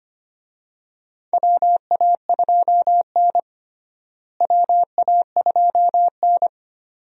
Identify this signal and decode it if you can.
Morse code, heard as a keyed tone, about 25 wpm, 700 Hz. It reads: WA2N WA2N
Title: USA Call Signs - 4 Characters - Repeated 2x - 25wpm